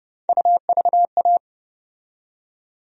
Morse code, heard as a keyed tone, about 30 wpm, 700 Hz. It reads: UVA
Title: USA Call Sign Postfixes - 3 Letters 30wpm